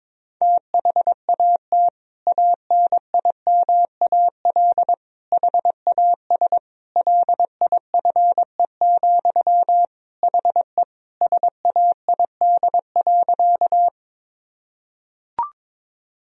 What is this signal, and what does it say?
Morse code, heard as a keyed tone, about 22 wpm, 700 Hz. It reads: THAT ANIMAL HAS LIFE, HE SAID.
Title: Sentences from Top 1000 Words - Review Part 1 - 22wpm